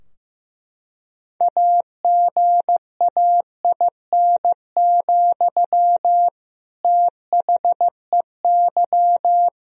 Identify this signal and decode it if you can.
Morse code, heard as a keyed tone, about 15 wpm, 700 Hz. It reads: AGAIN, THEY
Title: Sentences from Top 1000 Words - Review Part 2 - 15wpm